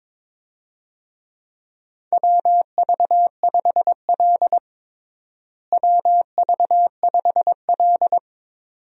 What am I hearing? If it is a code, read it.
Morse code, heard as a keyed tone, about 22 wpm, 700 Hz. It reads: WV5L WV5L
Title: USA Call Signs - 4 Characters - Repeated 2x - 22wpm